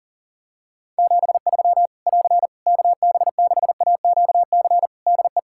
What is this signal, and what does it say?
Morse code, heard as a keyed tone, about 40 wpm, 700 Hz. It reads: 73 <AR> KB6AQC DE
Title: Practice QSO - Exchange 4 - Ending - 40wpm